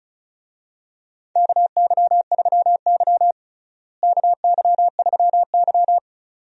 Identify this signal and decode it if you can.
Morse code, heard as a keyed tone, about 36 wpm, 700 Hz. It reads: KY3Y KY3Y
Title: USA Call Signs - 4 Characters - Repeated 2x - 35wpm